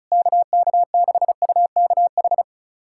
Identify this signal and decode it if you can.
Morse code, heard as a keyed tone, about 34 wpm, 700 Hz. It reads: KK6UKH
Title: USA Callsign DE Callsign - 6 Characters - 35wpm